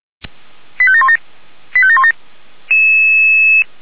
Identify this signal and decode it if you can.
Five-tone selective call (ZVEI), heard as quick runs of five tones, tones of about 70 ms. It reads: 85717 85717